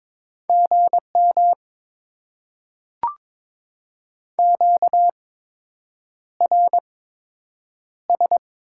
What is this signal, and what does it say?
Morse code, heard as a keyed tone, about 22 wpm, spaced as for 8 wpm, 700 Hz. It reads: GM QRS
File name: Top 90 QSO Elements - 8x Character Spacing - 22wpm